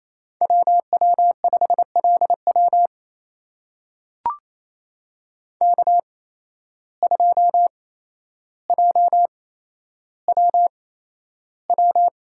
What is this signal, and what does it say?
Morse code, heard as a keyed tone, about 28 wpm, spaced as for 12 wpm, 700 Hz. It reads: WW5LW K2JWW
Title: USA Call Signs - 5 Characters - 8x Character Spacing - 28wpm